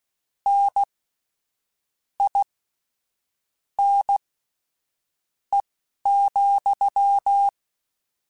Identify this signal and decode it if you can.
Morse code, heard as a keyed tone, about 16 wpm, spaced as for 6 wpm, 800 Hz. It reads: NINE,
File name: Misc / Misc-mp3s / WordPairs sp @10